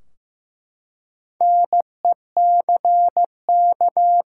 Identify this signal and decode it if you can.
Morse code, heard as a keyed tone, about 15 wpm, 700 Hz. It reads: NECK
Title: Four Letter Words - 15wpm